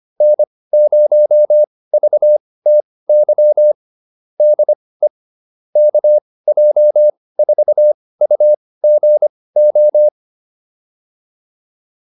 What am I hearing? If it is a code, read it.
Morse code, heard as a keyed tone, about 25 wpm, spaced as for 18 wpm, 600 Hz. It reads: N0VTY DE KJ4UGO